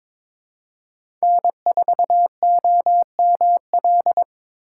Morse code, heard as a keyed tone, about 23 wpm, 700 Hz. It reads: N4OML
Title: USA Call Signs - 5 Characters - 3x Character Spacing - 22wpm